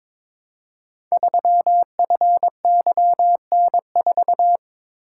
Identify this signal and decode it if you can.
Morse code, heard as a keyed tone, about 22 wpm, 700 Hz. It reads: 3FYN4
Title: Code Groups - 5 Letter:Numbers - 22wpm